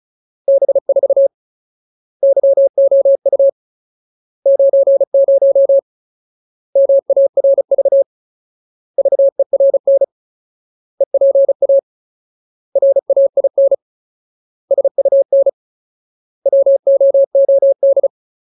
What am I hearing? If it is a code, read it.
Morse code, heard as a keyed tone, about 34 wpm, 550 Hz. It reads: B4 YOU 90 MARV VERN EPA RAIN SUN WOOD
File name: Ultimate Word List / 35WPM Ultimate